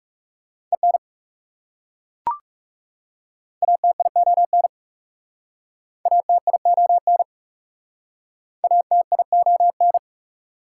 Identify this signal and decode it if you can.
Morse code, heard as a keyed tone, about 40 wpm, 690 Hz. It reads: EN ATION ATION ATION